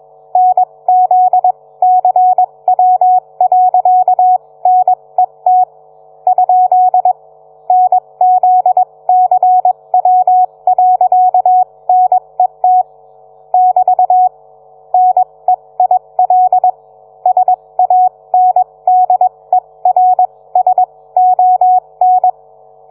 Morse code, heard as a keyed tone, about 21 wpm, spaced as for 18 wpm, 740 Hz. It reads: NZCW.NET ? NZCW.NET <BT> NEIL SANDERSON